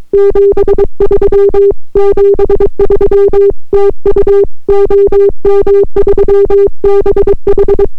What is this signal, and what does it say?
Morse code, heard as a keyed tone, about 22 wpm, 400 Hz. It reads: 73 73 TU OM3BH